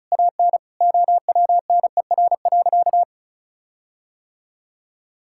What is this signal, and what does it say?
Morse code, heard as a keyed tone, about 34 wpm, 700 Hz. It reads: AN OWNER.
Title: Sentences from Top 1600 Words 35wpm